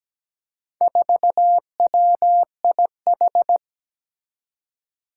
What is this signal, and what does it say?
Morse code, heard as a keyed tone, about 17 wpm, 700 Hz. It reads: 4WIH
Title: Four Challenging Letter-Number Combinations 17wpm